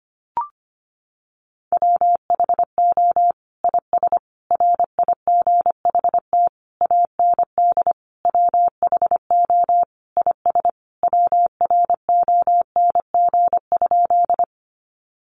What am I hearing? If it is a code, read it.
Morse code, heard as a keyed tone, about 25 wpm, 700 Hz. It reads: WHO IS RIGHT AND WHO IS WRONG?
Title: Sentences from Top 900 Words 25wpm